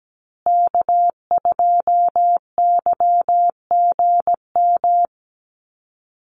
Morse code, heard as a keyed tone, about 17 wpm, 700 Hz. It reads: K2YGM
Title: Call Signs Repeated 2x - 17wpm